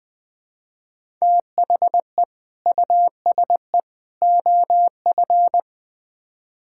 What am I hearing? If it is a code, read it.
Morse code, heard as a keyed tone, about 21 wpm, 700 Hz. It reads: THE USE OF
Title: Build A Phrase - 3-Word Phrases - 20wpm